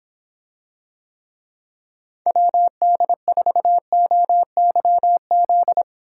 Morse code, heard as a keyed tone, about 26 wpm, 700 Hz. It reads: WD4OYZ